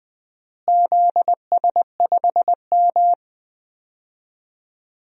Morse code, heard as a keyed tone, about 20 wpm, 700 Hz. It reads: ZS5M
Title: DXCC-100 Call Signs - 4 Characters - Repeated 2x - 20wpm